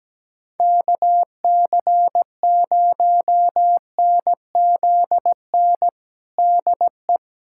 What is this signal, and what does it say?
Morse code, heard as a keyed tone, about 17 wpm, 700 Hz. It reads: KC0NZN DE
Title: USA Callsign DE Callsign 17wpm